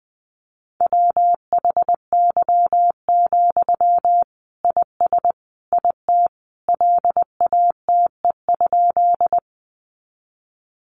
Morse code, heard as a keyed tone, about 20 wpm, 700 Hz. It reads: WHY, IS IT LATE?
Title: Sentences from Top 300 Words 20wpm